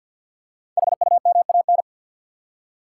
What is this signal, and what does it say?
Morse code, heard as a keyed tone, about 53 wpm, 700 Hz. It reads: HUMAN
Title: Five Letter Words - 50wpm